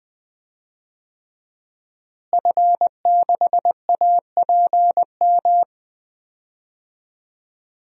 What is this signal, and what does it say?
Morse code, heard as a keyed tone, about 20 wpm, 700 Hz. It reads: F6APM
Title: DXCC-100 Call Signs - 5 Characters - Repeated 3x - 20wpm